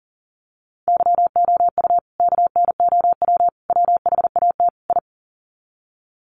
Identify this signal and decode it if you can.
Morse code, heard as a keyed tone, about 40 wpm, 700 Hz. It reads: YOU KNOW WHAT I
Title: Sets of 4 Words 40wpm